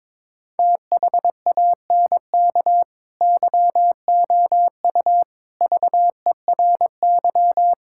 Morse code, heard as a keyed tone, about 22 wpm, 700 Hz. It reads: THANK YOU VERY
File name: practice / 22wpm / Build A Phrase - 5-Word Phrases - 22wpm